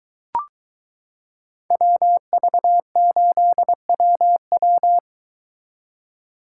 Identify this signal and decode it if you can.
Morse code, heard as a keyed tone, about 22 wpm, 700 Hz. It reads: WV8WW